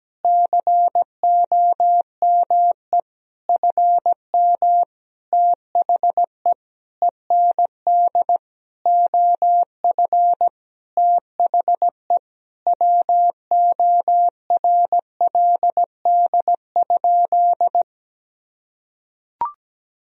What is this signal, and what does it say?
Morse code, heard as a keyed tone, about 17 wpm, 700 Hz. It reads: COME FM THE END OF THE WORLD?
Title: Sentences from Top 200 Words and Abbreviations 17wpm